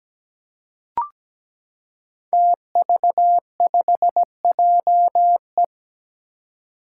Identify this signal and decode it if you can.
Morse code, heard as a keyed tone, about 17 wpm, 700 Hz. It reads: TV5JE
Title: Five Challenging Letter-Number Combinations 17wpm